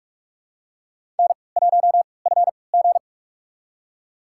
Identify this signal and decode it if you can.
Morse code, heard as a keyed tone, about 46 wpm, spaced as for 25 wpm, 700 Hz. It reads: N1FG
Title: USA Call Signs - 4 Characters - 3x Character Spacing - 45wpm